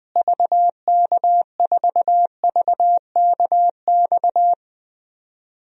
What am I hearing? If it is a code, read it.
Morse code, heard as a keyed tone, about 20 wpm, 700 Hz. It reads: VK4VKX
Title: DXCC-100 Call Signs - 6 Characters - Repeated 2x - 20wpm